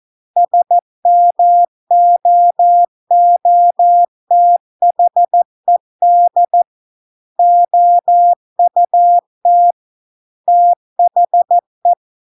Morse code, heard as a keyed tone, about 14 wpm, 700 Hz. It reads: SMOOTHED OUT THE